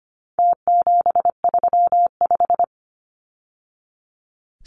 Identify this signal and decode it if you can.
Morse code, heard as a keyed tone, about 25 wpm, 700 Hz. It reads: T735
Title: Numbers - Contest Style - 4-Digit 25wpm